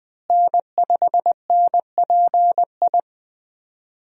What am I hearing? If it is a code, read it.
Morse code, heard as a keyed tone, about 20 wpm, 700 Hz. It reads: N5NPI
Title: USA Call Signs - 5 Characters - Repeated 2x - 20wpm